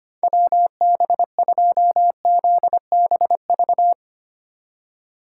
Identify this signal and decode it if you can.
Morse code, heard as a keyed tone, about 25 wpm, 700 Hz. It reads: WB2ZBV
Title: USA Call Signs - 6 Characters - 8x Character Spacing - 25wpm